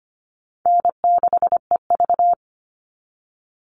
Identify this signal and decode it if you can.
Morse code, heard as a keyed tone, about 25 wpm, 700 Hz. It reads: N6EV
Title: Call Signs - 25wpm